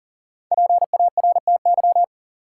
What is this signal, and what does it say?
Morse code, heard as a keyed tone, about 40 wpm, 700 Hz. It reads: PARTY
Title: Sentences from Top 1400 Words - New Word Drill Alt - 40wpm